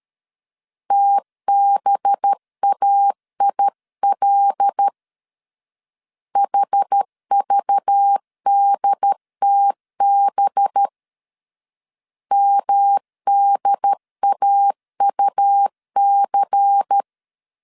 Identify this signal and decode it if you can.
Morse code, heard as a keyed tone, about 13 wpm, 800 Hz. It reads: TBAIL HVDTB MDAUC
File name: CW / DET / 1 6 1500C S B 12wpm 3431